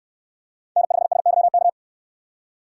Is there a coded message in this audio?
Yes: Morse code from 0.8 to 1.7 s, heard as a keyed tone, about 69 wpm, 700 Hz. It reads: N5SYK